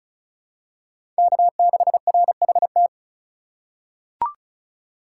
Morse code, heard as a keyed tone, about 36 wpm, 700 Hz. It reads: K6RHT
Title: USA Call Signs - 5 Characters - 4x Character Spacing - 35wpm